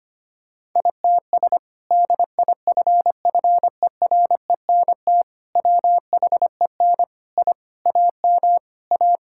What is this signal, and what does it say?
Morse code, heard as a keyed tone, about 25 wpm, 700 Hz. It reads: ITS DIFFERENT WHEN I AM A
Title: Sentences from Top 1000 Words - Review Part 5 - 25wpm